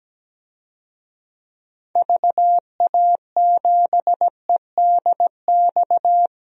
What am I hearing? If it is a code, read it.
Morse code, heard as a keyed tone, about 17 wpm, 700 Hz. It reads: VA7EDX
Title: DXCC-100 Call Signs - 6 Characters - 17wpm